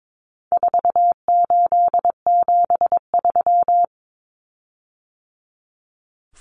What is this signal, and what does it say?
Morse code, heard as a keyed tone, about 22 wpm, 700 Hz. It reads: 4873
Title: Common Cut Numbers - 4-Digit 22wpm